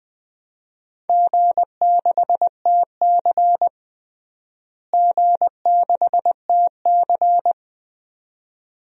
Morse code, heard as a keyed tone, about 20 wpm, 700 Hz. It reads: G6TC G6TC
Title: DXCC-100 Call Signs - 4 Characters - Repeated 2x - 20wpm